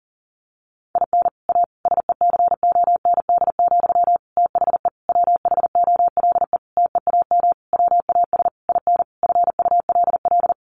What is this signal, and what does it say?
Morse code, heard as a keyed tone, about 40 wpm, 700 Hz. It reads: IN A SECOND, THE WHOLE TEAM WAS IN FULL
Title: Sentences from Top 1500 Words 40wpm